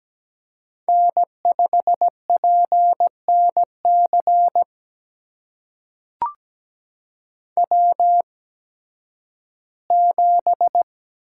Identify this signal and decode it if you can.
Morse code, heard as a keyed tone, about 17 wpm, spaced as for 11 wpm, 700 Hz. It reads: N5PNC W7
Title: USA Call Signs - 5 Characters - 8x Character Spacing - 17wpm